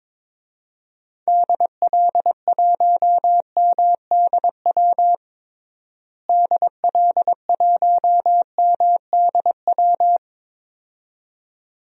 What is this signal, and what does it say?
Morse code, heard as a keyed tone, about 22 wpm, 700 Hz. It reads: DL1MDW DL1MDW
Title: DXCC-100 Call Signs - 6 Characters - Repeated 2x - 22wpm